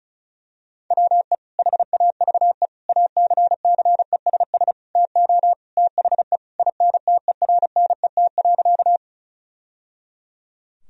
Morse code, heard as a keyed tone, about 36 wpm, 700 Hz. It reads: WE HAVE ACCESS TO THE INTERNET.
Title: Sentences from Top 1000 Words - Review Part 4 - 35wpm